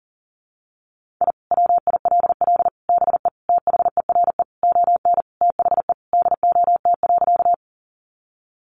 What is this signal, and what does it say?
Morse code, heard as a keyed tone, about 37 wpm, 700 Hz. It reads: I WILL BE THERE ON THE DOT.
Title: Sentences from Top 1500 Words 40wpm